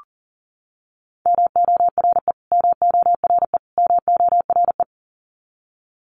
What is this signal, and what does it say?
Morse code, heard as a keyed tone, about 40 wpm, 700 Hz. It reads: MORE MORE MORE
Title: Top 300 Words Repeated 40wpm